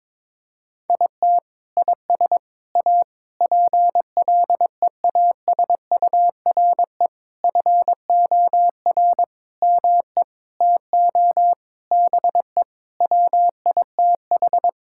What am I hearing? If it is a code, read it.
Morse code, heard as a keyed tone, about 22 wpm, 700 Hz. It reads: IT IS A PLEASURE FOR ME TO BE WITH
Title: Sentences from Top 1600 Words 22wpm